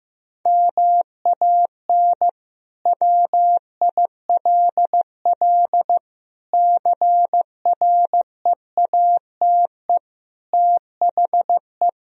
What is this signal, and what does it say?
Morse code, heard as a keyed tone, about 15 wpm, 700 Hz. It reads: MAN WILL CREATE THE
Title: Sentences from Top 1000 Words - Review Part 4 - 15wpm